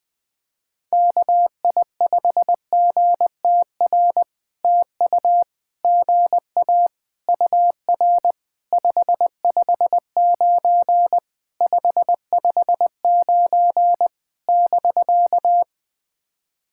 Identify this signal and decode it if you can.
Morse code, heard as a keyed tone, about 20 wpm, 700 Hz. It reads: KI5GTR TU GA UR 559 559 <BK>
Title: POTA QSO Practice - 20wpm